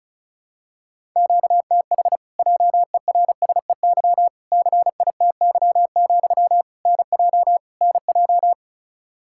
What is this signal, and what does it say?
Morse code, heard as a keyed tone, about 34 wpm, 700 Hz. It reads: QTH JERSEY CITY, NJ NJ
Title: Practice QSO - Exchange 1 - Beginning - 35wpm